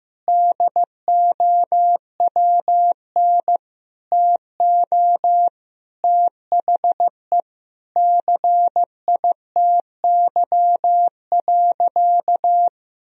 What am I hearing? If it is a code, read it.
Morse code, heard as a keyed tone, about 15 wpm, 700 Hz. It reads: DOWN TO THE CITY.
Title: Sentences from Top 1000 Words - Review Part 1 - 15wpm